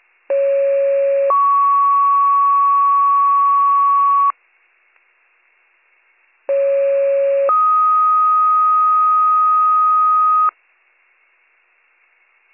Two-tone page, heard as a one-second tone, then a three-second tone, about 550 then 1120 Hz, about 550 then 1250 Hz. It is heard from 0.3 to 10.5 s.